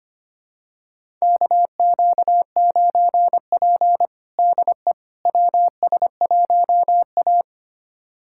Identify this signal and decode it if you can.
Morse code, heard as a keyed tone, about 25 wpm, 700 Hz. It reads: KQ9P DE WS1A
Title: USA Callsign DE Callsign - 4 Characters - 25wpm